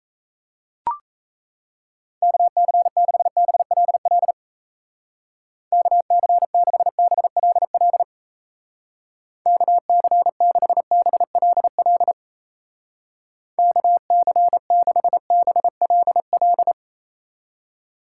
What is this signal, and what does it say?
Morse code, heard as a keyed tone, about 36 wpm, 700 Hz. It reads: KC6BLL KC6BLL KC6BLL KC6BLL